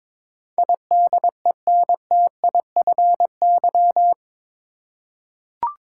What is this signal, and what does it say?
Morse code, heard as a keyed tone, about 22 wpm, 700 Hz. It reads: IDENTIFY
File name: practice / 22wpm / Eight Letter Words - 22wpm